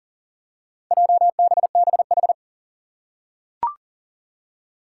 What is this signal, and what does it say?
Morse code, heard as a keyed tone, about 40 wpm, 700 Hz. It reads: JBBH
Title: Four Challenging Letter-Number Combinations 40wpm